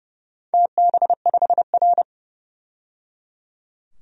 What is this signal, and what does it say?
Morse code, heard as a keyed tone, about 30 wpm, 700 Hz. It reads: TB5R
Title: DXCC-100 Call Signs - 4 Characters - 30wpm